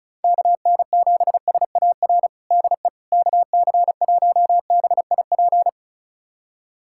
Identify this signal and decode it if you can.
Morse code, heard as a keyed tone, about 36 wpm, 700 Hz. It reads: KN7SAR DE KC1BIP
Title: USA Callsign DE Callsign 35wpm